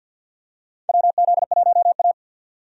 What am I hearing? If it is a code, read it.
Morse code, heard as a keyed tone, about 52 wpm, 700 Hz. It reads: WZ1A
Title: USA Call Signs - 4 Characters - 50wpm